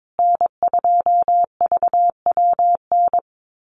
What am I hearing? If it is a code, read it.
Morse code, heard as a keyed tone, about 22 wpm, 700 Hz. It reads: N2VWN
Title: USA Call Signs Repeated 3x 22wpm